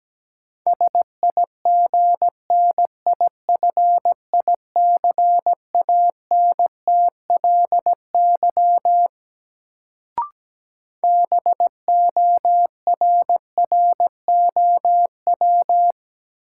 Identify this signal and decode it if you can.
Morse code, heard as a keyed tone, about 17 wpm, 700 Hz. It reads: SIGNIFICANTLY BORROW
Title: Sentences from Top 1500 Words - New Word Drill - 17wpm